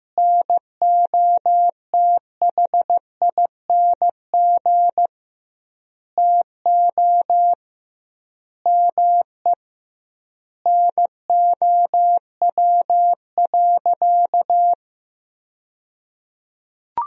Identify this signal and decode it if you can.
Morse code, heard as a keyed tone, about 15 wpm, 700 Hz. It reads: NOTHING TO ME NOW.
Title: 2x Word-Spacing - Sentences from Top 500 Words - 15wpm